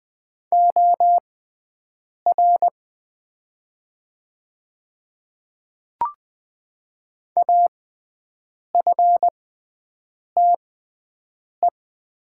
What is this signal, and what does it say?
Morse code, heard as a keyed tone, about 20 wpm, spaced as for 7 wpm, 700 Hz. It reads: OR AFTE
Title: 6x Character Spacing - Top 200 Words - 20wpm